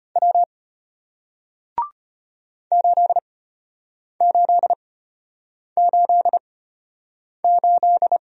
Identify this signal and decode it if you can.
Morse code, heard as a keyed tone, about 34 wpm, spaced as for 8 wpm, 700 Hz. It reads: W 8888